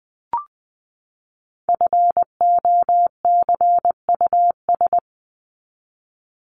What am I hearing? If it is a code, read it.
Morse code, heard as a keyed tone, about 20 wpm, 700 Hz. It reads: FOCUS
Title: Top 501-750 Words 20wpm